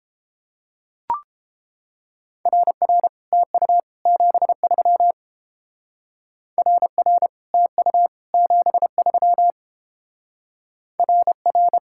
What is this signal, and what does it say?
Morse code, heard as a keyed tone, about 28 wpm, 700 Hz. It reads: RR TU 73 RR TU 73 RR